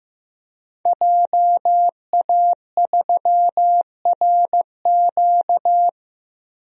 Morse code, heard as a keyed tone, about 15 wpm, 700 Hz. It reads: JA3RQ
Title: DXCC-100 Call Signs - 5 Characters - Repeated 3x - 15wpm